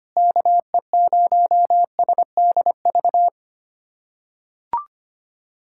Morse code, heard as a keyed tone, about 25 wpm, 700 Hz. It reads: KE0SDV
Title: USA Callsign DE Callsign 25wpm